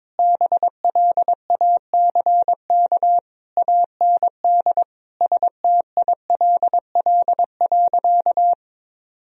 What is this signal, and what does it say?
Morse code, heard as a keyed tone, about 22 wpm, 700 Hz. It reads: BLACK AND STILL.
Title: Sentences from Top 1000 Words - Review Part 2 - 22wpm